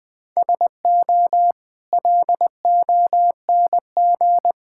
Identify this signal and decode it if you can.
Morse code, heard as a keyed tone, about 20 wpm, 700 Hz. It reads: SO LONG
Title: Sets of 8 Words - 20wpm